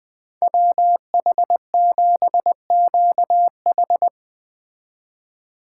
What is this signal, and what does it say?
Morse code, heard as a keyed tone, about 20 wpm, 700 Hz. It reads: WH7QH
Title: Speed Racing - USA 2x2 Call Signs 17wpm